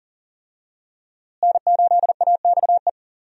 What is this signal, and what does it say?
Morse code, heard as a keyed tone, about 40 wpm, 700 Hz. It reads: N8AXE
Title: USA Call Signs - 5 Characters - Repeated 3x - 40wpm